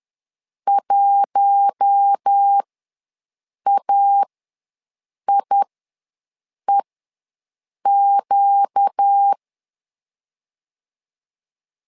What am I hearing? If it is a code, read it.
Morse code, heard as a keyed tone, about 10 wpm, spaced as for 6 wpm, 800 Hz. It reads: 1AIEQ
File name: LEZIONI / 4 Q1S3 ws 10wpm 2437